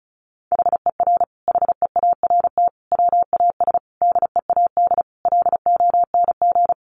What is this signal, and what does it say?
Morse code, heard as a keyed tone, about 34 wpm, 700 Hz. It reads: HER HEART WAS DEAD LONG
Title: Sentences from Top 1500 Words 35wpm